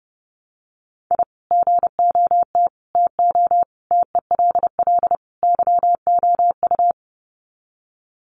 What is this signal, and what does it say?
Morse code, heard as a keyed tone, about 30 wpm, 700 Hz. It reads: I GOT TO TELL YOU
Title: Sets of 5 Words Encore2 30wpm